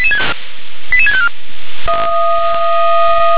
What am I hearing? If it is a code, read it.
Five-tone selective call (ZVEI), heard as quick runs of five tones, tones of about 70 ms. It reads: 88654
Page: ZVEI